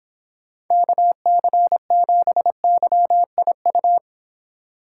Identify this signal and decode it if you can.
Morse code, heard as a keyed tone, about 26 wpm, 700 Hz. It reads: KC7YIU